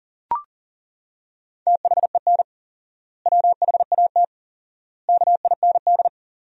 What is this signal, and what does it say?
Morse code, heard as a keyed tone, about 40 wpm, 700 Hz. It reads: THEN WHAT KIND
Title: 4x Word-Spacing - Sentences from Top 200 Words - 40wpm